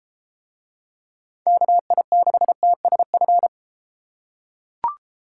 Morse code, heard as a keyed tone, about 31 wpm, 700 Hz. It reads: KI6TSF